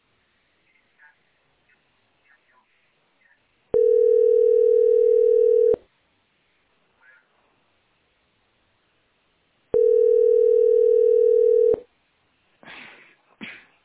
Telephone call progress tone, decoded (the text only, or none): ringback tone